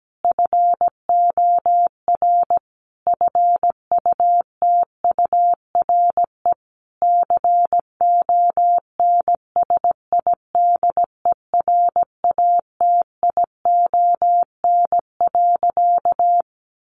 Morse code, heard as a keyed tone, about 17 wpm, 700 Hz. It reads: FOR FUTURE CONSIDERATION.
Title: Sentences from Top 1500 Words 17wpm